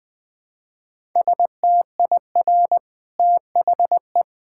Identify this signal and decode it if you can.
Morse code, heard as a keyed tone, about 21 wpm, 700 Hz. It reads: STIR THE
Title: Easy Sentences with Hints - 20wpm